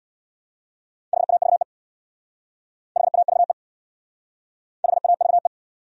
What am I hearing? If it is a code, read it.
Morse code, heard as a keyed tone, about 65 wpm, 700 Hz. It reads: HAVE HAVE HAVE